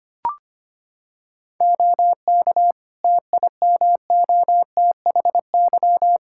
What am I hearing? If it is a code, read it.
Morse code, heard as a keyed tone, about 25 wpm, 700 Hz. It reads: OK TIMOTHY
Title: Practice QSO - Exchange 2 - Info - 25wpm